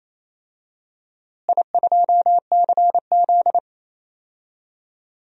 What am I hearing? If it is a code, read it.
Morse code, heard as a keyed tone, about 28 wpm, 700 Hz. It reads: I2CZ
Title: DXCC-100 Call Signs - 4 Characters - Repeated 2x - 28wpm